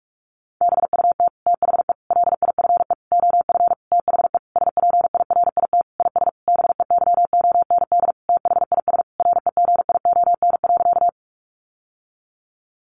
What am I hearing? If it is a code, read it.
Morse code, heard as a keyed tone, about 44 wpm, 700 Hz. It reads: BUT THE LIFE OF THE SPIRIT IS BEYOND THIS REGION.
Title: Sentences from Top 1500 Words 45wpm